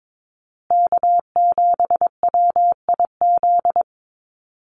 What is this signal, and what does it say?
Morse code, heard as a keyed tone, about 22 wpm, 700 Hz. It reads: K7WIZ
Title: USA Call Signs Repeated 2x 22wpm